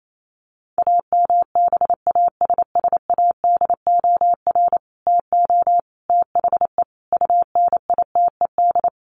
Morse code, heard as a keyed tone, about 28 wpm, 700 Hz. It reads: AMBASSADOR TO THE UNITED
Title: Sets of 5 Words Encore2 28wpm